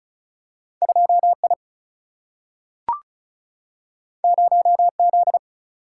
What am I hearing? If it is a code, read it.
Morse code, heard as a keyed tone, about 36 wpm, 700 Hz. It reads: 2 I 0 Z
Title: Single-Letter Single-Number 35wpm